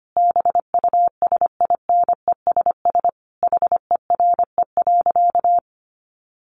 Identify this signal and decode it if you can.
Morse code, heard as a keyed tone, about 25 wpm, 700 Hz. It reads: BUSINESS HERE.
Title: Sentences from Top 1300 Words 25wpm